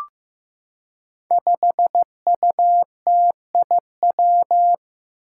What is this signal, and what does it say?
Morse code, heard as a keyed tone, about 15 wpm, 700 Hz. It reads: 5UTIW
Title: Five Challenging Letter-Number Combinations 15wpm